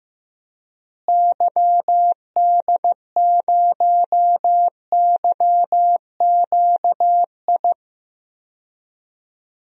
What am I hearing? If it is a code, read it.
Morse code, heard as a keyed tone, about 15 wpm, 700 Hz. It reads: YD0YQI
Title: DXCC-100 Call Signs - 6 Characters - Repeated 2x - 15wpm